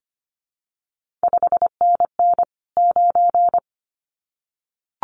Morse code, heard as a keyed tone, about 25 wpm, 700 Hz. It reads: 5NN 9
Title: CQ WW DX Contest Exchange - 25wpm